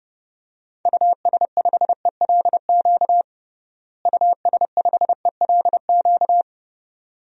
Morse code, heard as a keyed tone, about 32 wpm, 700 Hz. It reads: US5ELQ US5ELQ
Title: DXCC-100 Call Signs - 6 Characters - Repeated 2x - 30wpm